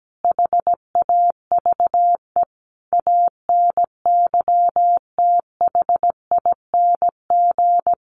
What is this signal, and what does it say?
Morse code, heard as a keyed tone, about 17 wpm, 700 Hz. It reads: HAVE ANYTHING
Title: Sentences from Top 1000 Words 17wpm